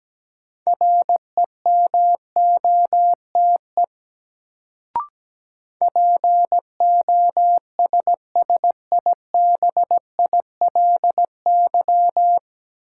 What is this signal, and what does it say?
Morse code, heard as a keyed tone, about 17 wpm, 700 Hz. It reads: REMOTE POSSIBILY
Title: Sentences from Top 1300 Words - New Word Drill - 17wpm